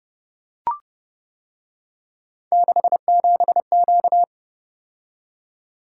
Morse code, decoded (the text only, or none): 67Q